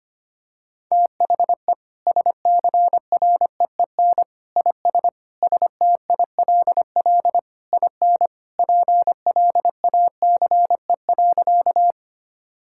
Morse code, heard as a keyed tone, about 25 wpm, 700 Hz. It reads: THE SCREEN IS STILL IN PLACE.
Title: Sentences from Top 1000 Words - Review Part 6 - 25wpm